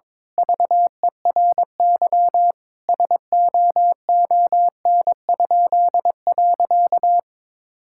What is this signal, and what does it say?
Morse code, heard as a keyed tone, about 22 wpm, 700 Hz. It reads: VERY SOON?.
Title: Pangram Challenge v2 - 22wpm